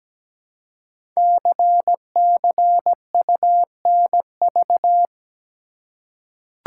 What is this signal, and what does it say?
Morse code, heard as a keyed tone, about 17 wpm, 700 Hz. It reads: CCUNV
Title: Code Groups - 5 Letters - 17wpm